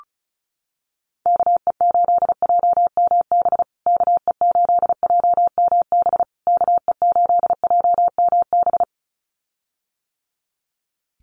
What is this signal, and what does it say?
Morse code, heard as a keyed tone, about 34 wpm, 700 Hz. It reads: KE8JMB KE8JMB KE8JMB
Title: USA Call Signs Repeated 3x 35wpm